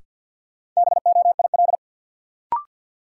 Morse code, heard as a keyed tone, about 53 wpm, 700 Hz. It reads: BOIL
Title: Sets of 5 Words Encore - New Word Drill - 50wpm